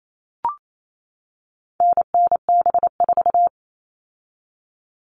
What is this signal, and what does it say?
Morse code, heard as a keyed tone, about 28 wpm, 700 Hz. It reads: NNB4
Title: All Cut Numbers - 4-Digit 28wpm